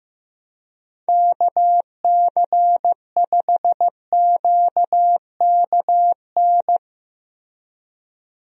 Morse code, heard as a keyed tone, about 15 wpm, 700 Hz. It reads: KC5QKN
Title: USA Call Signs - 6 Characters - Repeated 3x - 15wpm